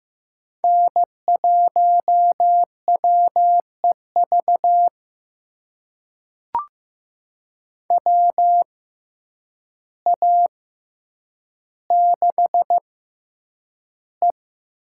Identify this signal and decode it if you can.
Morse code, heard as a keyed tone, about 15 wpm, spaced as for 8 wpm, 700 Hz. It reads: N1WEV WA6E
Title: USA Call Signs - 5 Characters - 6x Character Spacing - 15wpm